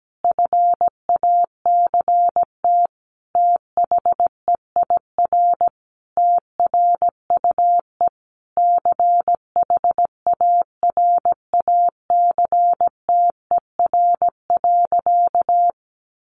Morse code, heard as a keyed tone, about 17 wpm, 700 Hz. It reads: FACT THEIR TRUE CHARACTER.
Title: Sentences from Top 1200 Words 17wpm